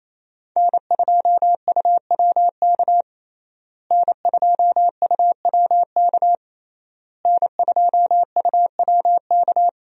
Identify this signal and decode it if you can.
Morse code, heard as a keyed tone, about 27 wpm, 700 Hz. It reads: N2UWK N2UWK N2UWK
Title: USA Call Signs - 5 Characters - Repeated 3x - 28wpm